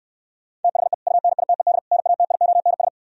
Morse code, heard as a keyed tone, about 67 wpm, 700 Hz. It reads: THE UNITED NATIONS